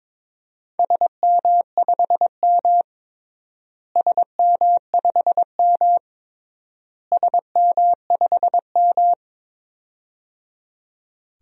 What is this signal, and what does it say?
Morse code, heard as a keyed tone, about 22 wpm, 700 Hz. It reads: SM5M SM5M SM5M
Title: DXCC-100 Call Signs - 4 Characters - Repeated 3x - 22wpm